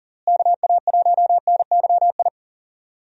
Morse code, heard as a keyed tone, about 39 wpm, 700 Hz. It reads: KA1NYI
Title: USA Call Signs - 6 Characters - Repeated 2x - 40wpm